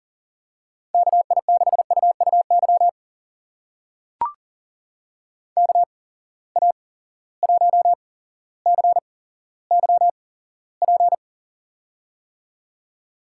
Morse code, heard as a keyed tone, about 38 wpm, spaced as for 17 wpm, 700 Hz. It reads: KI6UUY KA1CYP
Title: USA Call Signs - 8x Character Spacing - 40wpm